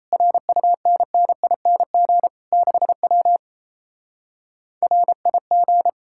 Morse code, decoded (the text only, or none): RUNNING 6W RIG